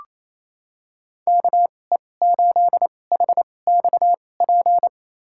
Morse code, heard as a keyed tone, about 28 wpm, spaced as for 20 wpm, 700 Hz. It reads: KE8HXP
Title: USA Call Signs - 6 Characters - 2x Character Spacing - 28wpm